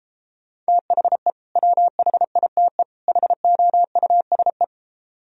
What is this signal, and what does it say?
Morse code, heard as a keyed tone, about 33 wpm, 700 Hz. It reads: THE WHITE HOUSE